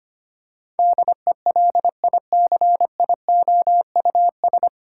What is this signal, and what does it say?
Morse code, heard as a keyed tone, about 25 wpm, 700 Hz. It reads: DELICIOUS
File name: Nine Letter Words - 25wpm